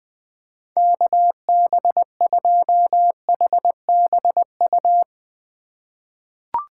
Morse code, heard as a keyed tone, about 21 wpm, 700 Hz. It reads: KB2HBU
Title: USA Call Signs - 6 Characters - 4x Character Spacing - 20wpm